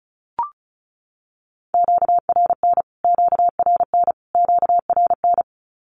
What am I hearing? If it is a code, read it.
Morse code, heard as a keyed tone, about 34 wpm, 700 Hz. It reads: QRN QRN QRN
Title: Q-Codes Repeated 3x 35wpm